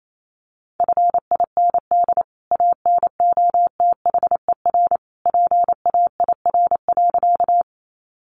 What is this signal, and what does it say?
Morse code, heard as a keyed tone, about 28 wpm, 700 Hz. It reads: FIND ANOTHER PAIR.
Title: Sentences from Top 1200 Words 28wpm